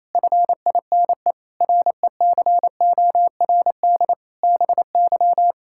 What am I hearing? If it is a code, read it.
Morse code, heard as a keyed tone, about 28 wpm, 700 Hz. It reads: FINE RECORD BY
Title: Pangram Challenge v2 - 28wpm